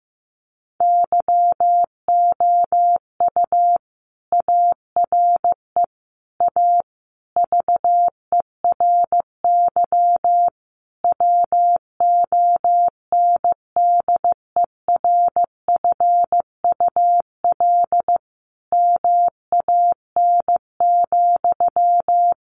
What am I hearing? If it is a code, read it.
Morse code, heard as a keyed tone, about 15 wpm, 700 Hz. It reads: YOU ARE A VERY WONDERFUL MAN,
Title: Sentences from Top 1100 Words 15wpm